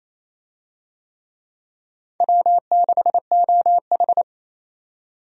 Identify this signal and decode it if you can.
Morse code, heard as a keyed tone, about 28 wpm, 700 Hz. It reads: W6OH
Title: USA Call Signs - 4 Characters - 28wpm